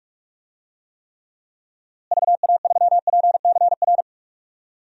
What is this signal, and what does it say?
Morse code, heard as a keyed tone, about 41 wpm, 700 Hz. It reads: VA3PCR
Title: DXCC-100 Call Signs - 6 Characters - Repeated 3x - 45wpm